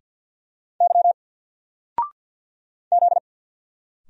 Morse code, heard as a keyed tone, about 50 wpm, 700 Hz. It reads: Y Z
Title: Single Letter-Number Alt 50wpm